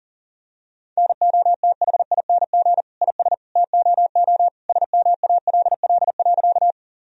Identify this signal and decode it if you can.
Morse code, heard as a keyed tone, about 40 wpm, 700 Hz. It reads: NOTHING IS TOO SMALL.
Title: Sentences from Top 1000 Words - Review Part 1 - 40wpm